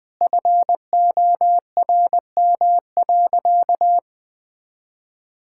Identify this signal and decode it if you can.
Morse code, heard as a keyed tone, about 20 wpm, 700 Hz. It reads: FORM.
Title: Sentences with 7-Word Phrases - 20wpm